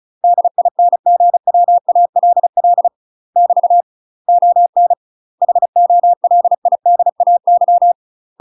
Morse code, heard as a keyed tone, about 35 wpm, 700 Hz. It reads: DINGWALL <BT> ON HOLIDAY
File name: UK MASTER QSOs TRAINING / 35WPM / UK MASTER QSOs 0042